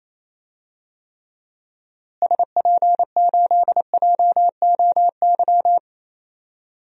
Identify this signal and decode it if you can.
Morse code, heard as a keyed tone, about 28 wpm, 700 Hz. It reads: SP8JOY
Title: DXCC-100 Call Signs - 6 Characters - Repeated 2x - 28wpm